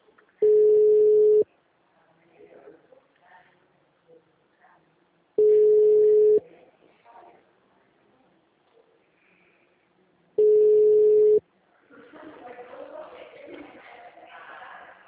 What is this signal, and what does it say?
Ringback tone (Europe, 425 Hz).